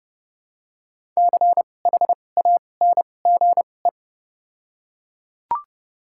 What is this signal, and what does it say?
Morse code, heard as a keyed tone, about 30 wpm, spaced as for 22 wpm, 700 Hz. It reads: CHANGE
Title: 2x Character Spacing - Top 200 Words - 30wpm